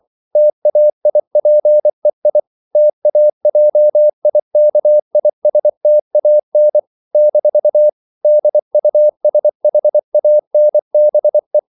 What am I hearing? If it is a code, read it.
Morse code, heard as a keyed tone, about 24 wpm, 600 Hz. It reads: TAIPEI TAJIKISTAN - DUSHANBE
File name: CW / Practice / NationalCapitals24